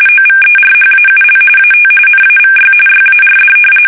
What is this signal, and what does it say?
RTTY, about 45.45 baud. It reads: OVERHAUL AT NAVY YARD
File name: fsk850 / battleshipnj